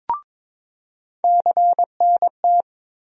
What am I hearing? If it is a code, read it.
Morse code, heard as a keyed tone, about 22 wpm, 700 Hz. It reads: CNT
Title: USA Call Sign Postfixes - 3 Letters 22wpm